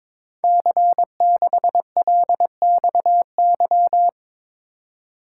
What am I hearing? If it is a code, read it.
Morse code, heard as a keyed tone, about 22 wpm, 700 Hz. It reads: C6LXY
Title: Code Groups - 5 Letter:Numbers - 22wpm